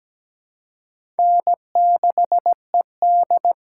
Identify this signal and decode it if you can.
Morse code, heard as a keyed tone, about 17 wpm, 700 Hz. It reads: N6ED
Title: USA Call Signs - 4 Characters - 3x Character Spacing - 17wpm